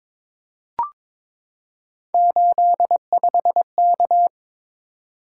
Morse code, heard as a keyed tone, about 22 wpm, 700 Hz. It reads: 85K
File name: practice / 22wpm / Three Letter-Number Combinations - 22wpm